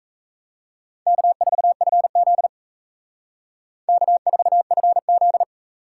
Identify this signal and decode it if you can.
Morse code, heard as a keyed tone, about 35 wpm, 700 Hz. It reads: K4FZ K4FZ